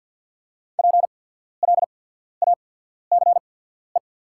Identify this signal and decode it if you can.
Morse code, heard as a keyed tone, about 48 wpm, spaced as for 14 wpm, 690 Hz. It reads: PLACE
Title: 8x Character Spacing - Top 500 Words - 50wpm